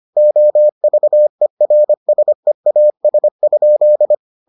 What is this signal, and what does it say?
Morse code, heard as a keyed tone, about 25 wpm, 600 Hz. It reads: OVERSEAS?